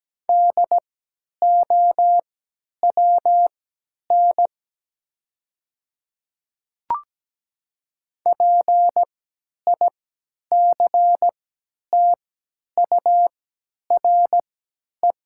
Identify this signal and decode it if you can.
Morse code, heard as a keyed tone, about 17 wpm, spaced as for 10 wpm, 700 Hz. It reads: DOWN PICTURE
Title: 3x Character Spacing - Top 200 Words - 17wpm